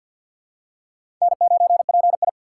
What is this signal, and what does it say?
Morse code, heard as a keyed tone, about 52 wpm, 700 Hz. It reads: N9PI
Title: USA Call Signs - 2x Character Spacing - 50wpm